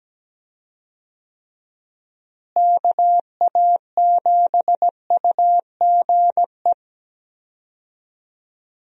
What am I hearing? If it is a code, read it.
Morse code, heard as a keyed tone, about 17 wpm, 700 Hz. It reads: KA7UGE
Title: USA Call Signs - 6 Characters - Repeated 3x - 17wpm